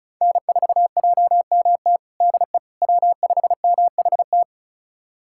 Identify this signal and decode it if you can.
Morse code, heard as a keyed tone, about 34 wpm, 700 Hz. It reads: N4JMT DE W5MHT
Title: USA Callsign DE Callsign - 5 Characters - 35wpm